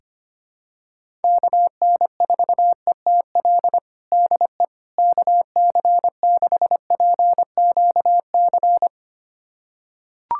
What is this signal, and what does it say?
Morse code, heard as a keyed tone, about 25 wpm, 700 Hz. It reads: KN4ETL DE KC6PQC
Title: USA Callsign DE Callsign 25wpm